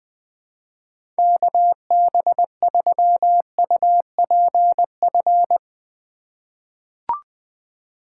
Morse code, heard as a keyed tone, about 21 wpm, 700 Hz. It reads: KB3UPF
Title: USA Call Signs - 6 Characters - 6x Character Spacing - 20wpm